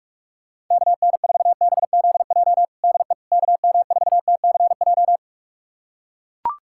Morse code, heard as a keyed tone, about 41 wpm, 700 Hz. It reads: KN4BZJ DE KM4TCJ
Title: USA Callsign DE Callsign - 6 Characters - 45wpm